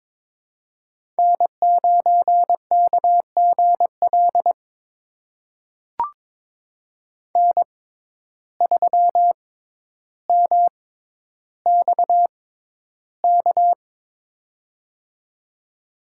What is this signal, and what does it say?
Morse code, heard as a keyed tone, about 21 wpm, spaced as for 11 wpm, 700 Hz. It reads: N9KGL N3MXK
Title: USA Call Signs - 5 Characters - 6x Character Spacing - 22wpm